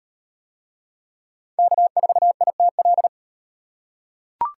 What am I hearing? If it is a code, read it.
Morse code, heard as a keyed tone, about 36 wpm, 700 Hz. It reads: K4ITL